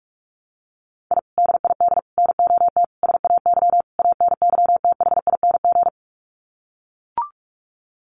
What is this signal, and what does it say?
Morse code, heard as a keyed tone, about 44 wpm, 690 Hz. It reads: I DID NOT SAY ANYTHING
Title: Sets of 5 Words Encore2 45wpm